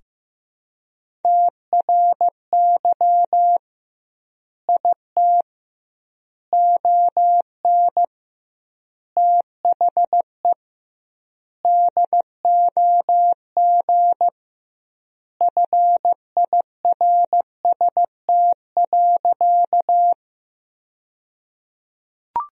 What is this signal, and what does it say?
Morse code, heard as a keyed tone, about 15 wpm, 700 Hz. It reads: TRY IT ON THE DOG FIRST.
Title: 2x Word-Spacing - Sentences from Top 400 Words - 15wpm